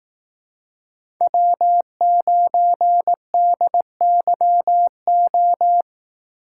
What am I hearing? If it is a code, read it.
Morse code, heard as a keyed tone, about 18 wpm, 700 Hz. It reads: W9DYO
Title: Speed Racing - USA 1x3 Call Signs 15wpm